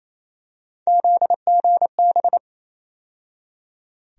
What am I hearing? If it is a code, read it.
Morse code, heard as a keyed tone, about 28 wpm, 700 Hz. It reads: ZGB
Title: USA Call Sign Postfixes - 3 Letters 28wpm